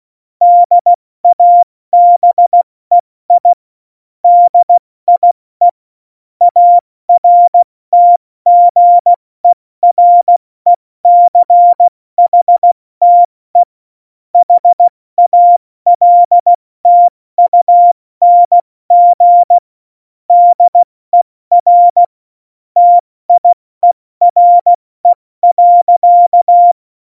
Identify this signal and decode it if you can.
Morse code, heard as a keyed tone, about 16 wpm, 700 Hz. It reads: DABEI DIE ARTGERECHTE HALTUNG DER TIERE.
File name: Morsekurs / Witze / Witz 14 Juli 2020, 70 F80